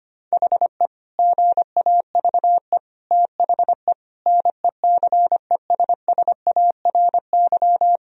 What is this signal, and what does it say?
Morse code, heard as a keyed tone, about 25 wpm, 700 Hz. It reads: HE GAVE THE NECESSARY
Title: Sentences from Top 1000 Words - Review Part 2 - 25wpm